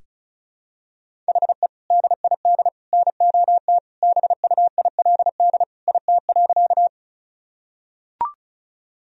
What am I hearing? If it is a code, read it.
Morse code, heard as a keyed tone, about 34 wpm, 700 Hz. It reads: HE DID NOT BUILD IT.
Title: Sentences from Top 1000 Words - Review Part 1 - 35wpm